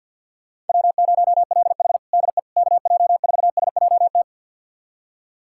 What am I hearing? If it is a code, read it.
Morse code, heard as a keyed tone, about 52 wpm, 700 Hz. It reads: W0LH DE KJ4SJT
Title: USA Callsign DE Callsign 50wpm